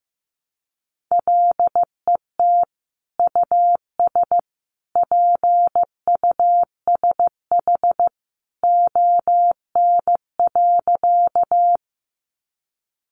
Morse code, heard as a keyed tone, about 15 wpm, 700 Hz. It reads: LET US PUSH ON.
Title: Sentences from Top 1300 Words 15wpm